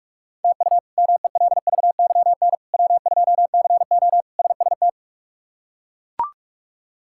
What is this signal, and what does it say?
Morse code, heard as a keyed tone, about 42 wpm, 700 Hz. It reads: TU MELVYN W2CO SST
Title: K1USN SST QSO Practice - 45wpm